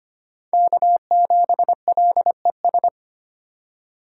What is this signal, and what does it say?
Morse code, heard as a keyed tone, about 25 wpm, 700 Hz. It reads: K7LES
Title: USA Call Signs - 5 Characters - 2x Character Spacing - 25wpm